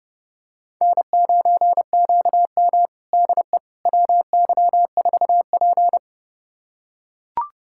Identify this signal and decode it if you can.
Morse code, heard as a keyed tone, about 30 wpm, 700 Hz. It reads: N9QM DE WY4P
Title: USA Callsign DE Callsign - 4 Characters - 30wpm